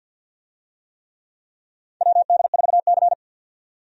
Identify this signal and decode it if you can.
Morse code, heard as a keyed tone, about 50 wpm, 700 Hz. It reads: WD4C
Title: USA Call Signs - 4 Characters - 50wpm